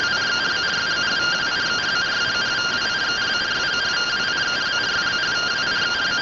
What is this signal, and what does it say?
Two-tone teleprinter signal, about 170 Hz shift.